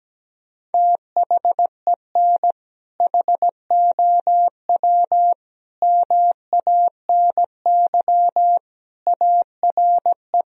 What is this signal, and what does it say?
Morse code, heard as a keyed tone, about 17 wpm, 700 Hz. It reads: THEN HOW MANY ARE
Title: Sentences from Top 1000 Words - Review Part 1 - 17wpm